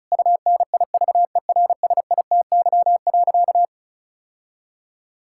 Morse code, heard as a keyed tone, about 34 wpm, 700 Hz. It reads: UNIVERSITY.
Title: Sentences from Top 1000 Words - Review Part 5 - 35wpm